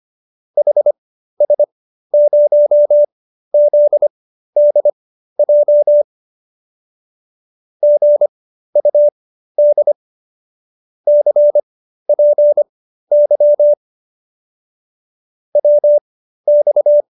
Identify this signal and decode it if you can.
Morse code, heard as a keyed tone, about 25 wpm, spaced as for 13 wpm, 590 Hz. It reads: HS0ZDJ GUD CPY WX